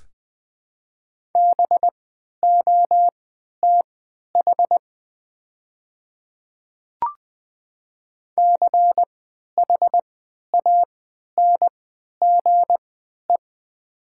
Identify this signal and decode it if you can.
Morse code, heard as a keyed tone, about 20 wpm, spaced as for 11 wpm, 700 Hz. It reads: BOTH CHANGE
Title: 3x Character Spacing - Top 300 Words - 20wpm